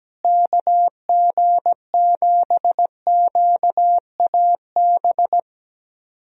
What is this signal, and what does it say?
Morse code, heard as a keyed tone, about 17 wpm, 700 Hz. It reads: KG7QAB
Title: USA Call Signs - 6 Characters - Repeated 3x - 17wpm